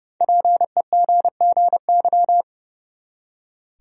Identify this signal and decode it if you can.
Morse code, heard as a keyed tone, about 29 wpm, 700 Hz. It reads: PEGGY
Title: USA Common Names - 30wpm